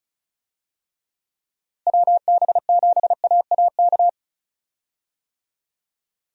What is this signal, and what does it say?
Morse code, heard as a keyed tone, about 36 wpm, 700 Hz. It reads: WB7AAK
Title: USA Call Signs - 6 Characters - Repeated 2x - 35wpm